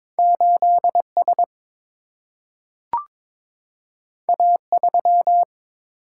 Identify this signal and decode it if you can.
Morse code, heard as a keyed tone, about 22 wpm, 700 Hz. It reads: 8S A3
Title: Single-Letter Single-Number 22wpm